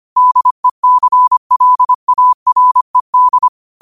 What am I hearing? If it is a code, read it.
Morse code, heard as a keyed tone, about 25 wpm, 1000 Hz. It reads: DECLARED
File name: qotdfiles / 25wpm